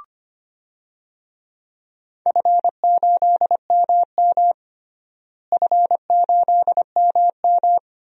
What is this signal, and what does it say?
Morse code, heard as a keyed tone, about 25 wpm, 700 Hz. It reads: F8MM F8MM
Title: DXCC-100 Call Signs - 4 Characters - Repeated 2x - 25wpm